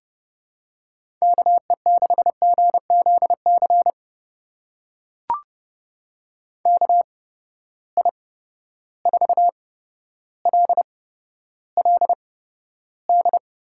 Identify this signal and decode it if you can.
Morse code, heard as a keyed tone, about 30 wpm, spaced as for 13 wpm, 700 Hz. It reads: KE6GZC KI4LLD
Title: USA Call Signs - 6 Characters - 8x Character Spacing - 30wpm